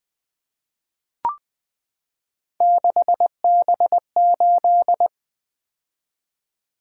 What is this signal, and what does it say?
Morse code, heard as a keyed tone, about 20 wpm, 700 Hz. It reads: 6B8
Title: Three Letter-Number Combinations - 20wpm